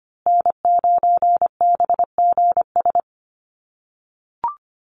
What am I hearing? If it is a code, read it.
Morse code, heard as a keyed tone, about 25 wpm, 700 Hz. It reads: N9BGS
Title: USA Call Signs Repeated 3x 25wpm